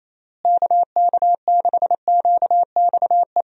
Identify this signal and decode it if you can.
Morse code, heard as a keyed tone, about 28 wpm, 700 Hz. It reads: KK6QXE
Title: USA Call Signs - 6 Characters - Repeated 2x - 28wpm